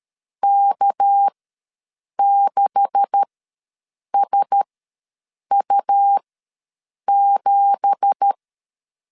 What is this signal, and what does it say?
Morse code, heard as a keyed tone, about 12 wpm, spaced as for 7 wpm, 800 Hz. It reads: K6SU7
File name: LEZIONI / 9 2867 ws 12wpm 3431